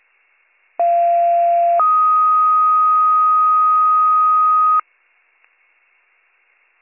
Two-tone page, heard as a one-second tone, then a three-second tone, about 690 then 1250 Hz. It is heard from 0.8 to 4.8 s.